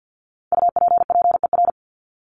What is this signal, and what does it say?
Morse code, heard as a keyed tone, about 50 wpm, 700 Hz. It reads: UPPER
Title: Top 1751-2000 Words 50wpm